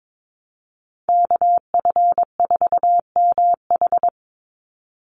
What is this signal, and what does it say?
Morse code, heard as a keyed tone, about 22 wpm, 700 Hz. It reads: KF4MH
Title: NA SOTA Chaser Callsigns 22wpm